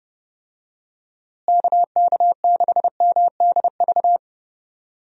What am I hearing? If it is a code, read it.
Morse code, heard as a keyed tone, about 30 wpm, 700 Hz. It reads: KK6MDV
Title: USA Call Signs - 6 Characters - Repeated 3x - 30wpm